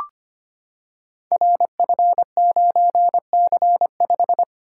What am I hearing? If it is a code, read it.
Morse code, heard as a keyed tone, about 25 wpm, 700 Hz. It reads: RF9C5
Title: Code Groups - 5 Letter:Numbers - 25wpm